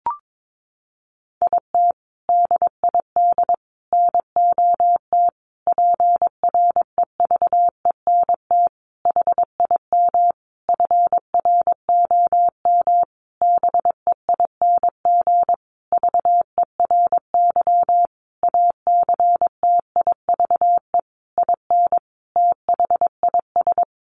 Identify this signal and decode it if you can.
Morse code, heard as a keyed tone, about 22 wpm, 700 Hz. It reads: IT DID NOT PREVENT HIM FROM BEING VERY ACTIVE IN THIS
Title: Sentences from Top 1500 Words 22wpm